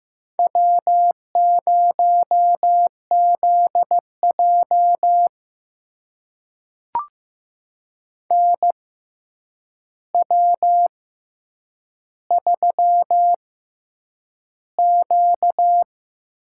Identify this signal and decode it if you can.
Morse code, heard as a keyed tone, about 15 wpm, spaced as for 8 wpm, 700 Hz. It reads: W0ZJ NW3Q
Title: USA Call Signs - 4 Characters - 6x Character Spacing - 15wpm